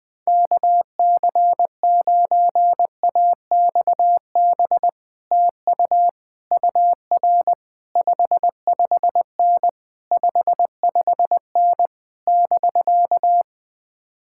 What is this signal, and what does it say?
Morse code, heard as a keyed tone, about 21 wpm, 700 Hz. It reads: KC9AXB TU UR 55N 55N <BK>
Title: Speed-Racing Common QSO Phrases Encore - 20wpm